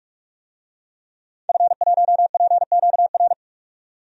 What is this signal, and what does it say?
Morse code, heard as a keyed tone, about 47 wpm, 700 Hz. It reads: F1PQR
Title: DXCC-100 Call Signs - 5 Characters - 45wpm